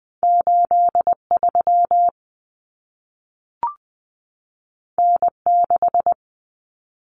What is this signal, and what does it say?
Morse code, heard as a keyed tone, about 20 wpm, 700 Hz. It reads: 83 N6
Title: Common Cut Numbers - 2-Digit 20wpm